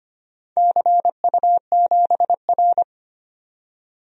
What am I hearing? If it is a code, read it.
Morse code, heard as a keyed tone, about 25 wpm, 700 Hz. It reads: CU7R
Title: DXCC-100 Call Signs - 4 Characters - 25wpm